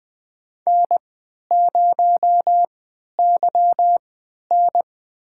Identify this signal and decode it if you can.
Morse code, heard as a keyed tone, about 20 wpm, spaced as for 11 wpm, 700 Hz. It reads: N0YN
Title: USA Call Signs - 4 Characters - 3x Character Spacing - 20wpm